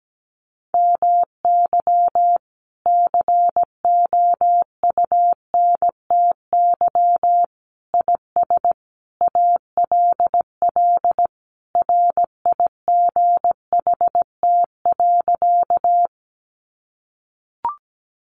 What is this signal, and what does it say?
Morse code, heard as a keyed tone, about 17 wpm, 700 Hz. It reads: MY COUNTY IS ALL RIGHT.
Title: Sentences from Top 1200 Words 17wpm